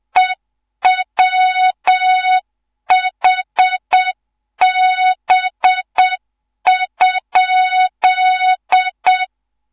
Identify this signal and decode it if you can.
Morse code, heard as a keyed tone, about 7 wpm, 3700 Hz, with noise. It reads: EWHB?
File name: Morze-HA2MN / hang / 14 280 HA2MN